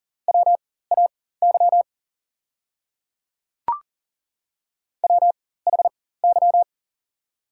Morse code, heard as a keyed tone, about 40 wpm, spaced as for 19 wpm, 690 Hz. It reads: WAY WHY
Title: 4x Character Spacing - Top 500 Words - 40wpm